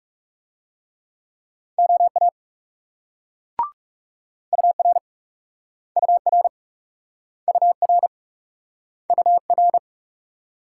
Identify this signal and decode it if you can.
Morse code, heard as a keyed tone, about 38 wpm, 700 Hz. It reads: OA UR UR UR UR